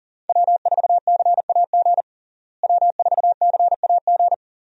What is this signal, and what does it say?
Morse code, heard as a keyed tone, about 40 wpm, 700 Hz. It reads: W4CAG W4CAG
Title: USA Call Signs - 5 Characters - Repeated 2x - 40wpm